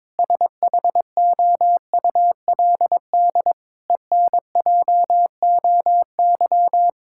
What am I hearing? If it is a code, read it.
Morse code, heard as a keyed tone, about 22 wpm, 700 Hz. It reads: SHOULD ENJOY
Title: Sentences from Top 1000 Words - Review Part 5 - 22wpm